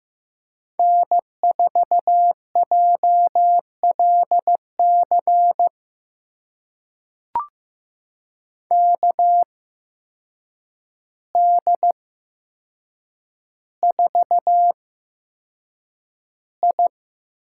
Morse code, heard as a keyed tone, about 15 wpm, spaced as for 7 wpm, 700 Hz. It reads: N4JLC KD4I
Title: USA Call Signs - 5 Characters - 8x Character Spacing - 15wpm